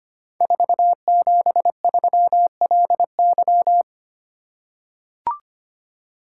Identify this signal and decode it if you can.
Morse code, heard as a keyed tone, about 25 wpm, 700 Hz. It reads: 473LY
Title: Code Groups - 5 Letter:Numbers - 25wpm